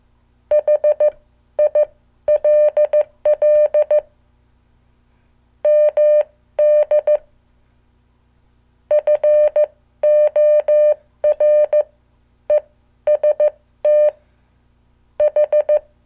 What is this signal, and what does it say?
Morse code, heard as a keyed tone, about 15 wpm, spaced as for 12 wpm, 600 Hz. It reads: HILL MD FOREST H
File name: morse / qso